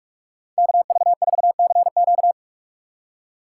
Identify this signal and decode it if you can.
Morse code, heard as a keyed tone, about 58 wpm, 700 Hz. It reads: KV4CQ